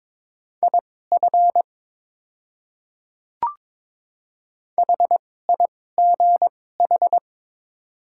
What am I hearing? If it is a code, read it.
Morse code, heard as a keyed tone, about 22 wpm, spaced as for 16 wpm, 700 Hz. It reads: IF HIGH
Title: 2x Character Spacing - Top 200 Words - 22wpm